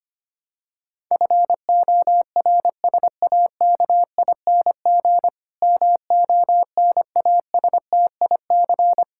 Morse code, heard as a keyed tone, about 25 wpm, 700 Hz. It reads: FORSAKING MONASTIC
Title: Pangram Challenge v2 - 25wpm